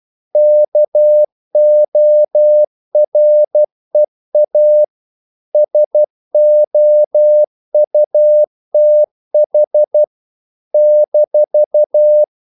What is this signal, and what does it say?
Morse code, heard as a keyed tone, about 12 wpm, 600 Hz. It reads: KOREA SOUTH -
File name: CW / Practice / NationalCapitals12